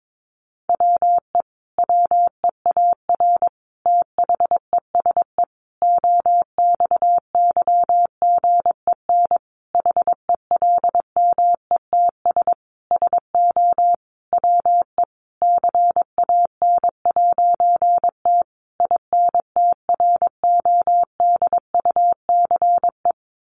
Morse code, heard as a keyed tone, about 22 wpm, 700 Hz. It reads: WE WEAR THESE OXYGEN HELMETS SO WE CAN'T INTRODUCE
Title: A Sound Of Thunder 22wpm